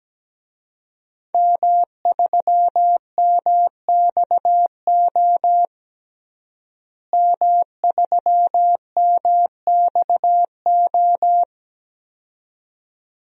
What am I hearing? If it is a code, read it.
Morse code, heard as a keyed tone, about 17 wpm, 700 Hz. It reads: M3MXO M3MXO
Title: DXCC-100 Call Signs - 5 Characters - Repeated 2x - 17wpm